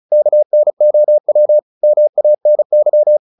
Morse code, heard as a keyed tone, about 34 wpm, 600 Hz. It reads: KNOW MANY
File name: the old man and the sea / the old man and the sea 1 35wpm